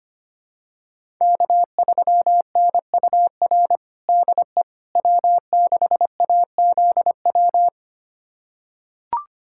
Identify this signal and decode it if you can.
Morse code, heard as a keyed tone, about 25 wpm, 700 Hz. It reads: K3NUR DE W6AZW
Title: USA Callsign DE Callsign - 5 Characters - 25wpm